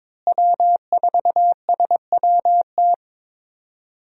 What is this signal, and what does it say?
Morse code, heard as a keyed tone, about 22 wpm, 700 Hz. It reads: W4SWT
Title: USA Call Signs - 5 Characters - Repeated 2x - 22wpm